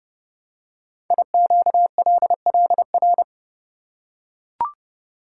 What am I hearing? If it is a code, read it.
Morse code, heard as a keyed tone, about 28 wpm, 700 Hz. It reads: IQLLR
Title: Code Groups - 5 Letters - 30wpm